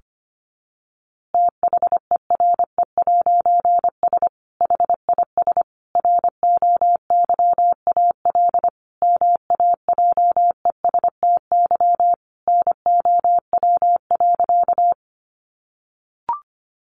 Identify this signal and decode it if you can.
Morse code, heard as a keyed tone, about 25 wpm, 700 Hz. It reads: THERE'S HIS ROYAL MAJESTY NOW.
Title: A Sound Of Thunder 25wpm